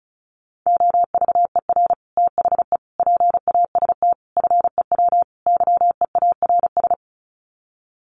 Morse code, heard as a keyed tone, about 34 wpm, 700 Hz. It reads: OVER THE PAST FEW YEARS
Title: Sets of 5 Words Encore 35wpm